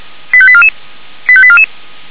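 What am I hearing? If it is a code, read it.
Five-tone selective call (ZVEI), heard as quick runs of five tones, tones of about 70 ms. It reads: 85844 85844
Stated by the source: ZVEI five-tone selective call received with an SDR.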